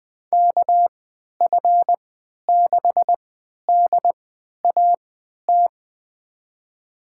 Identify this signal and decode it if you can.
Morse code, heard as a keyed tone, about 20 wpm, spaced as for 11 wpm, 700 Hz. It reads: KF6DAT
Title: USA Call Signs - 6 Characters - 3x Character Spacing - 20wpm